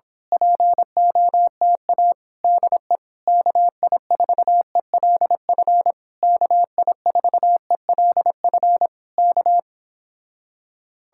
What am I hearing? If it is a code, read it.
Morse code, heard as a keyed tone, about 26 wpm, 700 Hz. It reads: POTA DE KI4ELF KI4ELF K
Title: Speed-Racing Common QSO Phrases - 22wpm